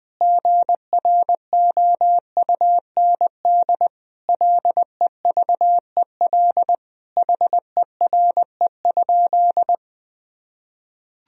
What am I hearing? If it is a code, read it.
Morse code, heard as a keyed tone, about 21 wpm, 700 Hz. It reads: GROUND LEVEL HERE?
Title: Sentences from Top 1000 Words - Review Part 4 - 20wpm